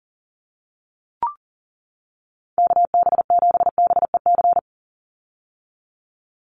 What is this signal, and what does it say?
Morse code, heard as a keyed tone, about 40 wpm, 700 Hz. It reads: KB7BEC
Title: USA Call Signs 40wpm